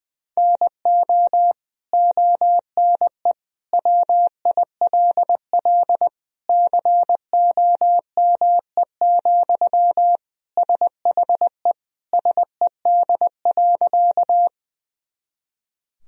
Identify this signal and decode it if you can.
Morse code, heard as a keyed tone, about 20 wpm, 700 Hz. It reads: NO ONE WILL COME, SHE SED.
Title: Sentences from Top 100 Words and Abbreviations 20wpm